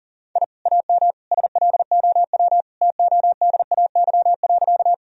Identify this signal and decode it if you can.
Morse code, heard as a keyed tone, about 39 wpm, 700 Hz. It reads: I AM SLOW TODAY.
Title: Sentences from Top 1000 Words - Review Part 2 - 40wpm